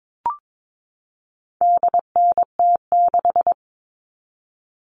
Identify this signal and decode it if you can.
Morse code, heard as a keyed tone, about 22 wpm, 700 Hz. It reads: DNT6
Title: All Cut Numbers - 4-Digit 22wpm